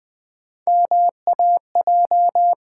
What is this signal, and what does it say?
Morse code, heard as a keyed tone, about 20 wpm, 700 Hz. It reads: MAJ
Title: Build A Word - 1 Letter At A Time - 5-Letter Words - 20wpm